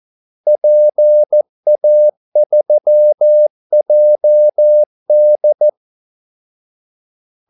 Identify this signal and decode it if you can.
Morse code, heard as a keyed tone, about 14 wpm, 600 Hz. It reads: PA3JD
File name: Callsigns-14wpm